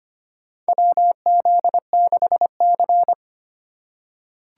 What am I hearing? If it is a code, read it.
Morse code, heard as a keyed tone, about 25 wpm, 700 Hz. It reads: WZ6C
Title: USA Call Signs - 4 Characters - 4x Character Spacing - 25wpm